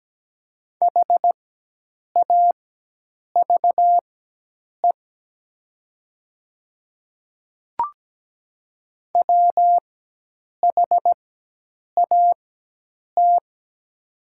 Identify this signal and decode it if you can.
Morse code, heard as a keyed tone, about 17 wpm, spaced as for 8 wpm, 700 Hz. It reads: HAVE WHAT
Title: 4x Character Spacing - Top 100 Words - 17wpm